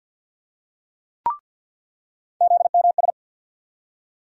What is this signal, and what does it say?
Morse code, heard as a keyed tone, about 47 wpm, 700 Hz. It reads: ZMS
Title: USA Call Sign Postfixes - 3 Letters 50wpm